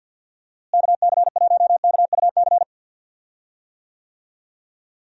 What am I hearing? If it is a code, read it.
Morse code, heard as a keyed tone, about 47 wpm, 700 Hz. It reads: KC1KUC
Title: USA Call Signs - 6 Characters - Repeated 3x - 50wpm